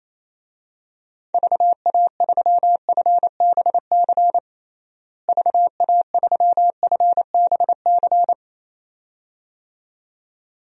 Morse code, heard as a keyed tone, about 27 wpm, 700 Hz. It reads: VA3FBC VA3FBC
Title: DXCC-100 Call Signs - 6 Characters - Repeated 2x - 28wpm